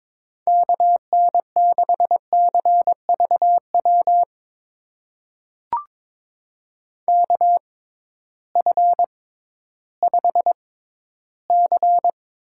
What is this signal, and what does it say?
Morse code, heard as a keyed tone, about 22 wpm, spaced as for 13 wpm, 700 Hz. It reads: KN6CVW KF5C
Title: USA Call Signs - 6 Characters - 6x Character Spacing - 22wpm